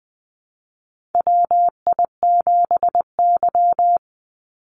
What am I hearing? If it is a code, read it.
Morse code, heard as a keyed tone, about 20 wpm, 700 Hz. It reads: WI7Y
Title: USA Call Signs 20wpm